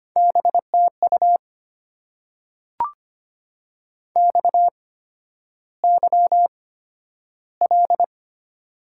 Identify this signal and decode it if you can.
Morse code, heard as a keyed tone, about 25 wpm, spaced as for 11 wpm, 700 Hz. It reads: BTU XYL
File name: Top 90 QSO Elements - 8x Character Spacing - 25wpm